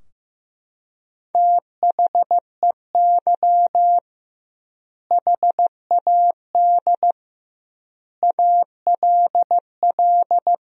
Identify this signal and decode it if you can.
Morse code, heard as a keyed tone, about 16 wpm, 700 Hz. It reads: THEY HAD ALL
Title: 2x Word-Spacing - Sentences from Top 200 Words - 15wpm